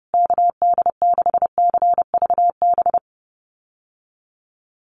Morse code, heard as a keyed tone, about 30 wpm, 700 Hz. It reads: KD6CVB
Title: USA Call Signs Repeated 3x 30wpm